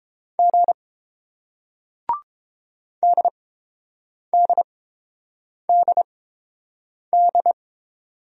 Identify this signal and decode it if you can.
Morse code, heard as a keyed tone, about 30 wpm, spaced as for 7 wpm, 700 Hz. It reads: G DDDD